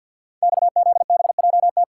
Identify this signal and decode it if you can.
Morse code, heard as a keyed tone, about 51 wpm, 700 Hz. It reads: XZBJT
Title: Code Groups - 5 Letter:Numbers - 50wpm